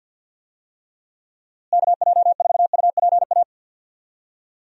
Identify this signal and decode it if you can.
Morse code, heard as a keyed tone, about 53 wpm, 700 Hz. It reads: KJ4UPA
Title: USA Call Signs - 6 Characters - Repeated 2x - 50wpm